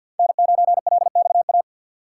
Morse code, heard as a keyed tone, about 52 wpm, 700 Hz. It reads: N9LXA
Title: USA Call Signs - 5 Characters - 4x Character Spacing - 50wpm